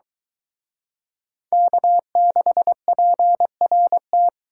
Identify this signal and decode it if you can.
Morse code, heard as a keyed tone, about 23 wpm, 700 Hz. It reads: K6PRT